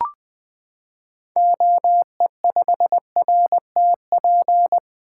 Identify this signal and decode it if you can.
Morse code, heard as a keyed tone, about 21 wpm, 700 Hz. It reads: OE5RTP
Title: 2023 SOTA Activator Call Signs - 20wpm